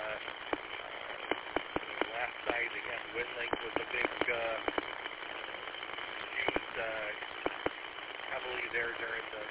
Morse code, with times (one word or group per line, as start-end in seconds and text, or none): none